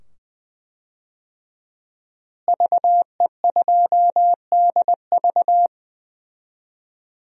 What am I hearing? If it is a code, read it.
Morse code, heard as a keyed tone, about 20 wpm, 700 Hz. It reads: VE2DV
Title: DXCC-100 Call Signs - 5 Characters - Repeated 2x - 20wpm